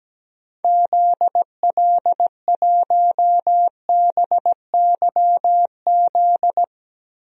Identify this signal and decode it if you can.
Morse code, heard as a keyed tone, about 17 wpm, 700 Hz. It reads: ZL1BYZ
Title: DXCC-100 Call Signs - 6 Characters - Repeated 2x - 17wpm